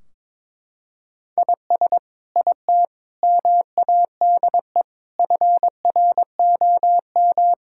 Morse code, heard as a keyed tone, about 22 wpm, 700 Hz. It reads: IS IT MADE FROM
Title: Sentences from Top 1600 Words 22wpm